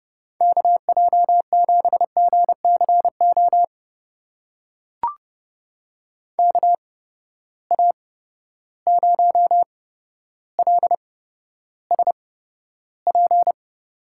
Morse code, heard as a keyed tone, about 30 wpm, spaced as for 13 wpm, 700 Hz. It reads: KJ7GCO KA0LSP
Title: USA Call Signs - 6 Characters - 8x Character Spacing - 30wpm